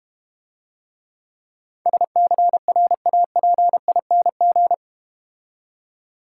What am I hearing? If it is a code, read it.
Morse code, heard as a keyed tone, about 30 wpm, 700 Hz. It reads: SCRAPING
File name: Easy Words - Speed-Racing with Hints - 25wpm